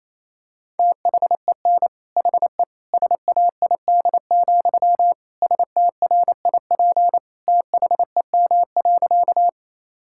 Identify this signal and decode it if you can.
Morse code, heard as a keyed tone, about 28 wpm, 700 Hz. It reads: THEN HE SAID, STRIP THEM.
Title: Sentences from Top 1600 Words 28wpm